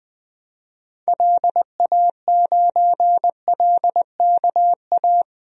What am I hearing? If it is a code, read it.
Morse code, heard as a keyed tone, about 20 wpm, 700 Hz. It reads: LA9LKA
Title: DXCC-100 Call Signs - 6 Characters - Repeated 2x - 20wpm